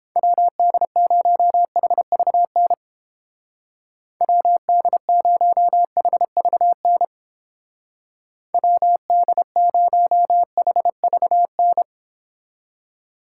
Morse code, heard as a keyed tone, about 31 wpm, 700 Hz. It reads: WD0HVN WD0HVN WD0HVN